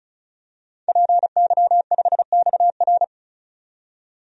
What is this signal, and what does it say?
Morse code, heard as a keyed tone, about 34 wpm, 700 Hz. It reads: PY5XR
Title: DXCC-100 Call Signs - 5 Characters - 35wpm